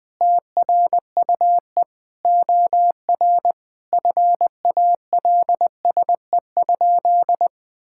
Morse code, heard as a keyed tone, about 20 wpm, 700 Hz. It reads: TRUE OR FALSE?
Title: Sentences from Top 1600 Words 20wpm